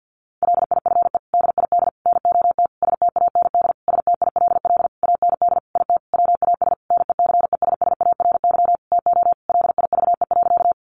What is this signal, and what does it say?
Morse code, heard as a keyed tone, about 50 wpm, 700 Hz. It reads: LIFE DID NOT STAND STILL AND IT WAS NECESSARY TO LIVE.
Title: Sentences from Top 800 Words 50wpm